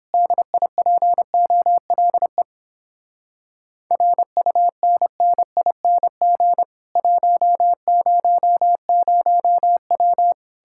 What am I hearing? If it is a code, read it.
Morse code, heard as a keyed tone, about 31 wpm, 700 Hz. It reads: DIPOLE RUNNING 100W